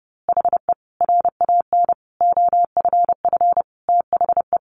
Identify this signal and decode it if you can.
Morse code, heard as a keyed tone, about 30 wpm, 700 Hz. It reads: HE RAN OFF THE
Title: A Sound Of Thunder 30wpm